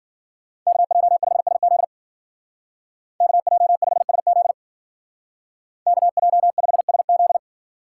Morse code, heard as a keyed tone, about 53 wpm, 700 Hz. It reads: KJ5SZ KJ5SZ KJ5SZ